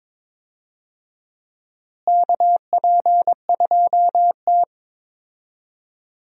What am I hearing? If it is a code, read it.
Morse code, heard as a keyed tone, about 21 wpm, 700 Hz. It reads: KP2T
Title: USA Call Signs - 4 Characters - Repeated 3x - 22wpm